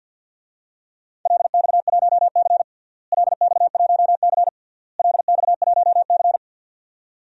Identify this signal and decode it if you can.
Morse code, heard as a keyed tone, about 52 wpm, 700 Hz. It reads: LX1C LX1C LX1C
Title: DXCC-100 Call Signs - 4 Characters - Repeated 3x - 50wpm